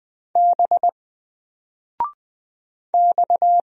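Morse code, heard as a keyed tone, about 20 wpm, 700 Hz. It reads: B X
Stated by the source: Single letter and number Morse drill at 20 wpm, from N4LCD.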